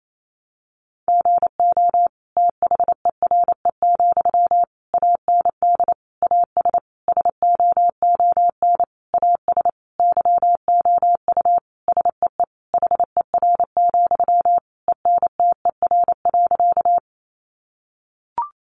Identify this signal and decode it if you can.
Morse code, heard as a keyed tone, about 28 wpm, 700 Hz. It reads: GO THERE, AND AS SOON AS YOU SEE HER, ENTER.
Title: Sentences from Top 1300 Words 28wpm